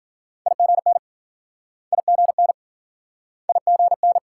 Morse code, heard as a keyed tone, about 50 wpm, 700 Hz. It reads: IGN IGN IGN